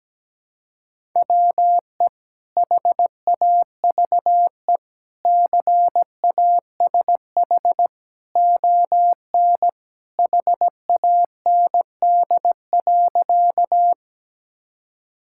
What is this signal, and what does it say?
Morse code, heard as a keyed tone, about 17 wpm, 700 Hz. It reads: WE HAVE CASH ON HAND.
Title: Sentences from Top 1000 Words - Review Part 5 - 17wpm